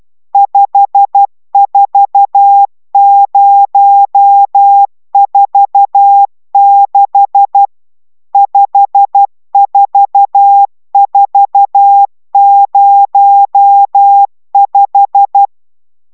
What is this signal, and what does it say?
Morse code, heard as a keyed tone, about 12 wpm, 800 Hz. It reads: 54046 54405